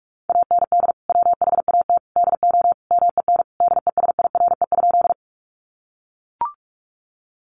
Morse code, heard as a keyed tone, about 44 wpm, 700 Hz. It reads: AND WHAT DO MEN DESIRE?
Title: Sentences from Top 1100 Words 45wpm